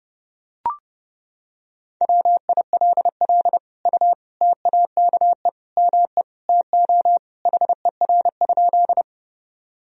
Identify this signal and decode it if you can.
Morse code, heard as a keyed tone, about 30 wpm, 700 Hz. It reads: WILL U TAKE ME TO HER?
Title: Sentences from Top 200 Words and Abbreviations 30wpm